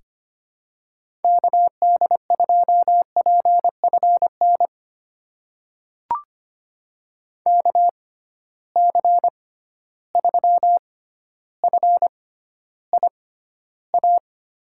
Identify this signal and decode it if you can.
Morse code, heard as a keyed tone, about 25 wpm, spaced as for 13 wpm, 700 Hz. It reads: KD2PFN KC3FIA
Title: USA Call Signs - 6 Characters - 6x Character Spacing - 25wpm